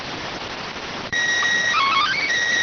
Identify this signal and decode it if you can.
SSTV (slow-scan television), PD 290.